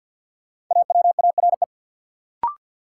Morse code, heard as a keyed tone, about 50 wpm, 700 Hz. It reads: AWARE
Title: Sentences from Top 1200 Words - New Word Drill - 50wpm